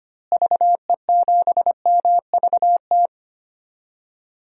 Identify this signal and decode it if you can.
Morse code, heard as a keyed tone, about 25 wpm, 700 Hz. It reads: VE7MVT
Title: DXCC-100 Call Signs - 6 Characters - Repeated 2x - 25wpm